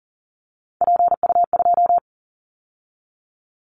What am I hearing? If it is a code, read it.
Morse code, heard as a keyed tone, about 40 wpm, 700 Hz. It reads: PU2
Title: Callsign Prefixes for Top 50 DXCC - 40wpm